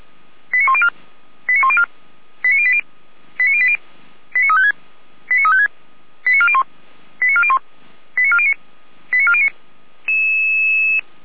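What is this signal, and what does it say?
Five-tone selective call (ZVEI), heard as quick runs of five tones, tones of about 70 ms. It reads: 89104 89104 89080 89080 89356 89356 89401 89401 89409 89409
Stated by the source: Five-tone ZVEI selective call 89350, received by SDR.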